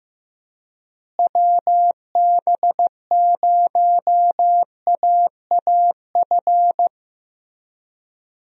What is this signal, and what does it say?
Morse code, heard as a keyed tone, about 15 wpm, 700 Hz. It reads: WB0AAF
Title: USA Call Signs - 6 Characters - Repeated 3x - 15wpm